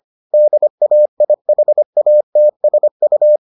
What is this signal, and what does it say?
Morse code, heard as a keyed tone, about 25 wpm, 600 Hz. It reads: DAIHATSU